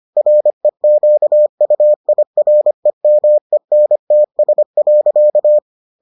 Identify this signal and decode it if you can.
Morse code, heard as a keyed tone, about 25 wpm, 600 Hz. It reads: REQUIREMENTS.